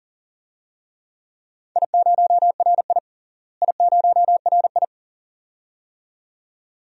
Morse code, heard as a keyed tone, about 40 wpm, 700 Hz. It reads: I0RI I0RI
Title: DXCC-100 Call Signs - 4 Characters - Repeated 2x - 40wpm